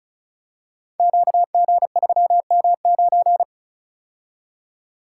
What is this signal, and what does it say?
Morse code, heard as a keyed tone, about 36 wpm, 700 Hz. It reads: QG3M9
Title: Code Groups - 5 Letter:Numbers - 35wpm